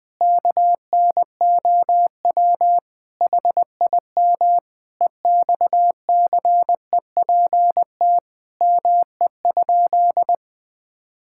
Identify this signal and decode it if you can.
Morse code, heard as a keyed tone, about 21 wpm, 700 Hz. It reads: KNOW HIM EXCEPT ME?
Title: Sentences from Top 1600 Words 20wpm